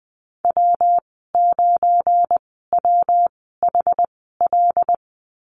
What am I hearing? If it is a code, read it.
Morse code, heard as a keyed tone, about 20 wpm, spaced as for 14 wpm, 700 Hz. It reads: W9WHL
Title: 186 Lesson 30g - Letter b - Call Signs 20wpm